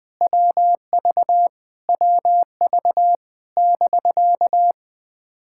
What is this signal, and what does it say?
Morse code, heard as a keyed tone, about 20 wpm, 700 Hz. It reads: WV WV <BK>
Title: POTA QSO Practice - 20wpm